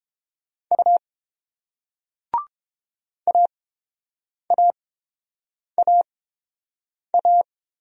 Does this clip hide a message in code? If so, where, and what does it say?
Yes: Morse code from 0.7 to 7.4 s, heard as a keyed tone, about 30 wpm, spaced as for 7 wpm, 700 Hz. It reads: U AAAA